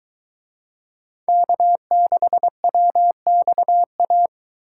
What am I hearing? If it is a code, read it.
Morse code, heard as a keyed tone, about 22 wpm, 700 Hz. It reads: K6WXA